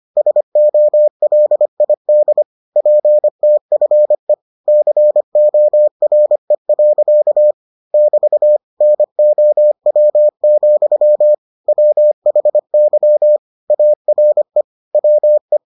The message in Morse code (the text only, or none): SOLID PTFE CORE. <BT> NOW, WHY ARE WE